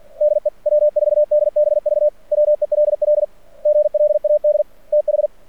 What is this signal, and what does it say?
Morse code, heard as a keyed tone, about 48 wpm, 600 Hz. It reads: DE W4DBV WELL GLAD TH
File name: archived / CRSnet